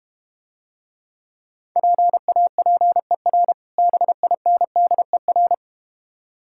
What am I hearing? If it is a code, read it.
Morse code, heard as a keyed tone, about 32 wpm, 700 Hz. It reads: PAPER BINDER